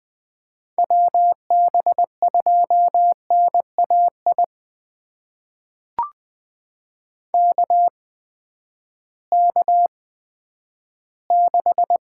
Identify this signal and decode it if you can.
Morse code, heard as a keyed tone, about 20 wpm, spaced as for 11 wpm, 700 Hz. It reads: WB2NAI KK6
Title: USA Call Signs - 6 Characters - 8x Character Spacing - 20wpm